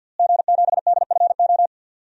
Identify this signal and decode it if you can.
Morse code, heard as a keyed tone, about 52 wpm, 700 Hz. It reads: G7DFO
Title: DXCC-100 Call Signs - 5 Characters - Repeated 2x - 50wpm